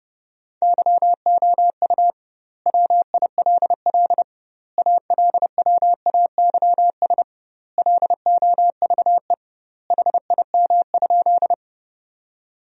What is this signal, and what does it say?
Morse code, heard as a keyed tone, about 32 wpm, 700 Hz. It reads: YOU WILL ALWAYS LOVE HIM?
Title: 2x Word-Spacing - Sentences from Top 400 Words - 30wpm